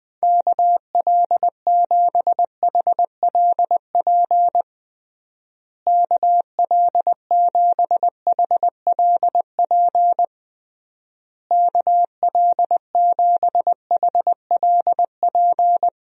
Morse code, heard as a keyed tone, about 21 wpm, 700 Hz. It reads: KL7HLP KL7HLP KL7HLP
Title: DXCC-100 Call Signs - 6 Characters - Repeated 3x - 20wpm